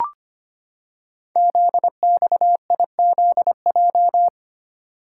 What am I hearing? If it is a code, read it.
Morse code, heard as a keyed tone, about 25 wpm, 700 Hz. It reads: ZXIZJ
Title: Code Groups - 5 Letters - 25wpm